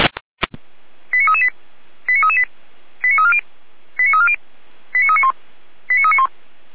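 Five-tone selective call (ZVEI), heard as quick runs of five tones, tones of about 70 ms. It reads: 89338 89338 89340 89340 89391 89391
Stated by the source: ZVEI five-tone selective call received by SDR.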